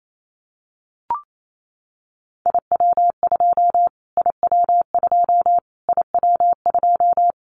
Morse code, heard as a keyed tone, about 28 wpm, 700 Hz. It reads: IW2 IW2 IW2
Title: Callsign Prefixes for Top 25 DXCC - Repeated 3x - 28wpm